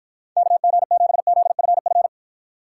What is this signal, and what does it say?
Morse code, heard as a keyed tone, about 52 wpm, 700 Hz. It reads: KG7ZFF